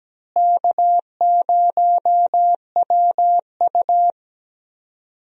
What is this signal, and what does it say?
Morse code, heard as a keyed tone, about 17 wpm, 700 Hz. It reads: K0WU
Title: USA Call Signs - 4 Characters - Repeated 2x - 17wpm